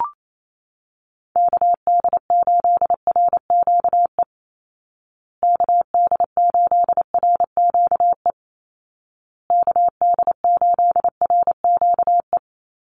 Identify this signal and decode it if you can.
Morse code, heard as a keyed tone, about 28 wpm, 700 Hz. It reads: KD8RQE KD8RQE KD8RQE
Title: Call Signs Repeated 2x - 28wpm